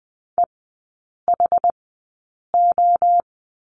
Morse code, heard as a keyed tone, about 20 wpm, spaced as for 8 wpm, 700 Hz. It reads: EHO
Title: 067 Lesson 12e - Letter d - Three Characters 20wpm